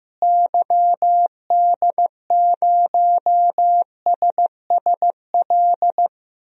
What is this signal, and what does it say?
Morse code, heard as a keyed tone, about 15 wpm, 700 Hz. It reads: YD0SSL
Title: DXCC-100 Call Signs - 6 Characters - Repeated 2x - 15wpm